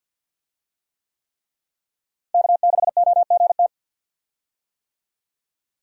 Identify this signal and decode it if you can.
Morse code, heard as a keyed tone, about 46 wpm, 690 Hz. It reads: K6OGT
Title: USA Call Signs - 5 Characters - Repeated 3x - 50wpm